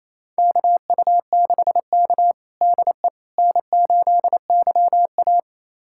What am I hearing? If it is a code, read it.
Morse code, heard as a keyed tone, about 28 wpm, 700 Hz. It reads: KU6K DE N8YA
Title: USA Callsign DE Callsign - 4 Characters - 28wpm